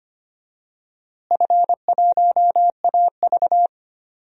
Morse code, heard as a keyed tone, about 25 wpm, 700 Hz. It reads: F1AV
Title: DXCC-100 Call Signs - 4 Characters - Repeated 3x - 25wpm